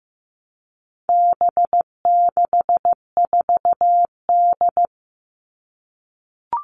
Morse code, heard as a keyed tone, about 15 wpm, 700 Hz. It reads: B64D
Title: All Cut Numbers - 4-Digit 15wpm